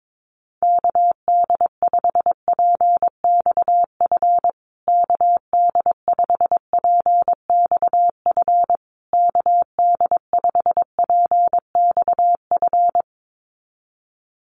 Morse code, heard as a keyed tone, about 22 wpm, 700 Hz. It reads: KD5PXF KD5PXF KD5PXF
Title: USA Call Signs Repeated 3x 22wpm